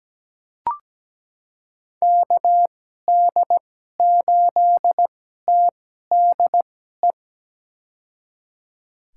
Morse code, heard as a keyed tone, about 17 wpm, spaced as for 12 wpm, 700 Hz. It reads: KD8TDE
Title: USA Call Signs - 6 Characters - 2x Character Spacing - 17wpm